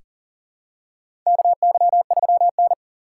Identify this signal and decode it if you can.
Morse code, heard as a keyed tone, about 40 wpm, 700 Hz. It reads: KY3N
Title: USA Call Signs - 4 Characters - 6x Character Spacing - 40wpm